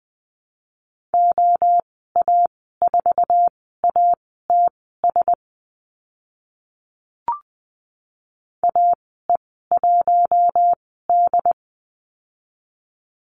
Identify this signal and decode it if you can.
Morse code, heard as a keyed tone, about 20 wpm, spaced as for 14 wpm, 700 Hz. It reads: OA4ATS AE1D
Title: 082 Lesson 14g - Number 2 - Call Signs 20wpm